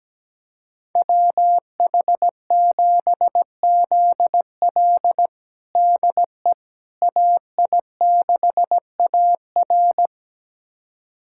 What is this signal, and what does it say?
Morse code, heard as a keyed tone, about 17 wpm, 700 Hz. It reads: WH7ZL DE AI6AR
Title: USA Callsign DE Callsign - 5 Characters - 17wpm